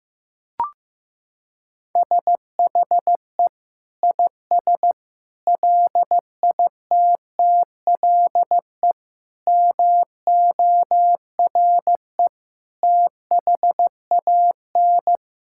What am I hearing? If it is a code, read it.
Morse code, heard as a keyed tone, about 15 wpm, 700 Hz. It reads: SHE IS LITTLE MORE THAN
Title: Sentences from Top 1600 Words 15wpm